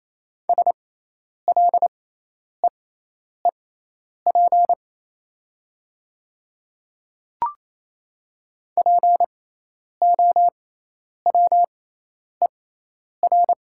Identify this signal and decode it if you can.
Morse code, heard as a keyed tone, about 28 wpm, spaced as for 10 wpm, 690 Hz. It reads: SLEEP POWER
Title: 6x Character Spacing - Top 500 Words - 28wpm